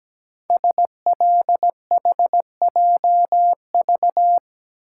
Morse code, heard as a keyed tone, about 17 wpm, 700 Hz. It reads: SLHJV
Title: Code Groups - 5 Letters - 17wpm